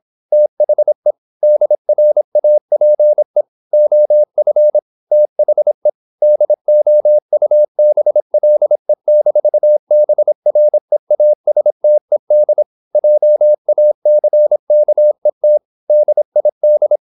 Morse code, heard as a keyed tone, about 26 wpm, 600 Hz. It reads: THE DRAPE OF THE DOUBLE-BREASTED JACKET DID
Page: Never 26wpm | CW med Gnister